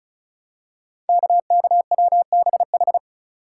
Morse code, heard as a keyed tone, about 36 wpm, 690 Hz. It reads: KKWBH
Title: Code Groups - 5 Letters - 35wpm